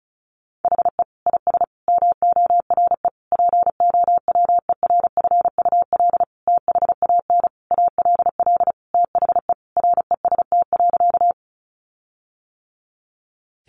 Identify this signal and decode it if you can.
Morse code, heard as a keyed tone, about 34 wpm, 700 Hz. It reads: HE IS MORE POWERFUL THAN ALL THE REST.
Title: Sentences from Top 1300 Words 35wpm